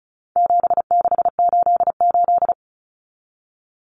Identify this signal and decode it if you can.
Morse code, heard as a keyed tone, about 34 wpm, 700 Hz. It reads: 7688
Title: Common Cut Numbers - 4-Digit 35wpm